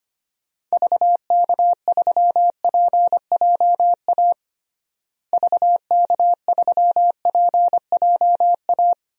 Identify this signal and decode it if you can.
Morse code, heard as a keyed tone, about 25 wpm, 700 Hz. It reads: VK3PJA VK3PJA
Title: DXCC-100 Call Signs - 6 Characters - Repeated 2x - 25wpm